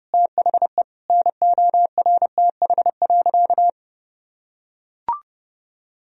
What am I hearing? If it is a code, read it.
Morse code, heard as a keyed tone, about 32 wpm, 700 Hz. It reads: THE NORTH.
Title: Sentences from Top 1000 Words - Review Part 1 - 30wpm